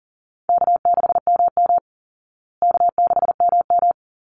Morse code, heard as a keyed tone, about 37 wpm, 700 Hz. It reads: K6MM K6MM
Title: Call Signs Repeated 3x - 40wpm